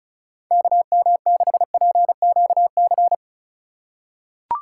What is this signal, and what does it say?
Morse code, heard as a keyed tone, about 34 wpm, 700 Hz. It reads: KM6PQC
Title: USA Call Signs - 6 Characters - 8x Character Spacing - 35wpm